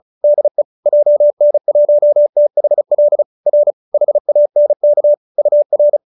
Morse code, heard as a keyed tone, about 34 wpm, 590 Hz. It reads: DE JN1THL R HANK UR